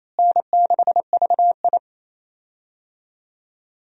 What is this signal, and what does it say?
Morse code, heard as a keyed tone, about 27 wpm, 700 Hz. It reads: N6VI
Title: USA Call Signs - 4 Characters - Repeated 2x - 28wpm